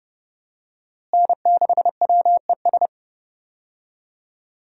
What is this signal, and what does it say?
Morse code, heard as a keyed tone, about 31 wpm, 700 Hz. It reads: N6WES
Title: USA Call Signs - 5 Characters - Repeated 3x - 30wpm